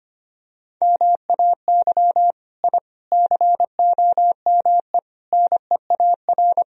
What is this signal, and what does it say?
Morse code, heard as a keyed tone, about 25 wpm, 700 Hz. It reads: MAY I COME NEAR
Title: Sentences from Top 1000 Words - Review Part 1 - 25wpm